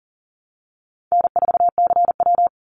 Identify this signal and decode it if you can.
Morse code, heard as a keyed tone, about 40 wpm, 700 Hz. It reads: N4CW
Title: Easy Call Signs Repeated 2x - 40wpm